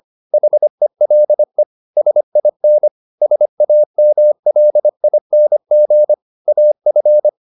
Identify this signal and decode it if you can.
Morse code, heard as a keyed tone, about 25 wpm, 600 Hz. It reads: HELE SIN SAMLING AF
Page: Ferie 25 wpm